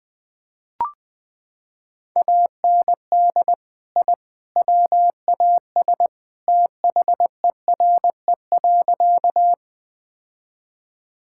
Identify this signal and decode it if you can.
Morse code, heard as a keyed tone, about 21 wpm, 700 Hz. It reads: AND I WAS THERE.
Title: Sentences with 3-Word Phrases - 20wpm